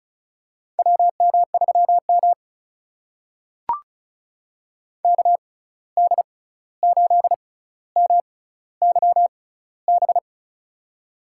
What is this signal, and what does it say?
Morse code, heard as a keyed tone, about 36 wpm, spaced as for 16 wpm, 700 Hz. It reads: WM3M KD8MYB
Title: USA Call Signs - 6x Character Spacing - 35wpm